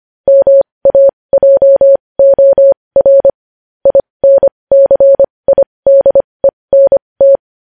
Morse code, heard as a keyed tone, about 25 wpm, spaced as for 20 wpm, 550 Hz. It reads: MAJOR INCIDENT
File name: podcast / News headlines-1x-25x20